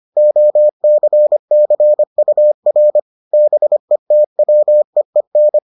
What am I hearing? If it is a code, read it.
Morse code, heard as a keyed tone, about 25 wpm, 600 Hz. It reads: OCCUR BETWEEN